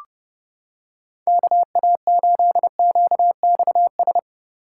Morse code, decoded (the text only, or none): KA8QXS